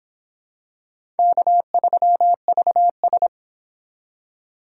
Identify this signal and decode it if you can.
Morse code, heard as a keyed tone, about 25 wpm, 700 Hz. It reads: K3VS